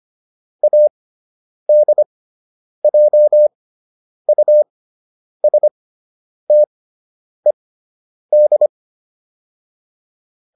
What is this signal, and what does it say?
Morse code, heard as a keyed tone, about 25 wpm, spaced as for 9 wpm, 600 Hz. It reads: ADJUSTED